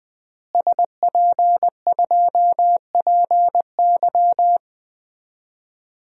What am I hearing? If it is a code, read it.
Morse code, heard as a keyed tone, about 20 wpm, 700 Hz. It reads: SP2PY
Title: DXCC-100 Call Signs - 5 Characters - Repeated 2x - 20wpm